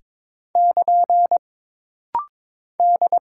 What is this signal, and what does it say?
Morse code, heard as a keyed tone, about 23 wpm, 700 Hz. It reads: <KN> D
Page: This rapid-fire timing is geared to push your ICR (Instant Character Recognition) skills.